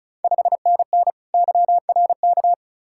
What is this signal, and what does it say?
Morse code, heard as a keyed tone, about 36 wpm, 690 Hz. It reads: 5NN YRK
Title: Ontario QSO Party Exchange 35wpm